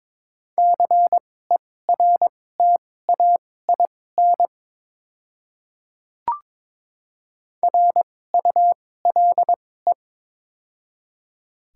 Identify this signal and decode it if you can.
Morse code, heard as a keyed tone, about 22 wpm, spaced as for 16 wpm, 700 Hz. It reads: CERTAIN RULE
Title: 2x Character Spacing - Top 500 Words - 22wpm